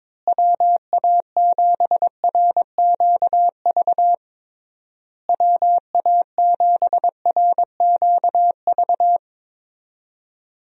Morse code, heard as a keyed tone, about 22 wpm, 700 Hz. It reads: WA7RQV WA7RQV
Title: USA Call Signs - 6 Characters - Repeated 3x - 22wpm